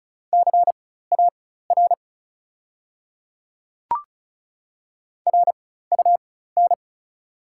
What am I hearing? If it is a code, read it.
Morse code, heard as a keyed tone, about 36 wpm, spaced as for 16 wpm, 690 Hz. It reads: CAR RUN
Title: 4x Character Spacing - Top 300 Words - 35wpm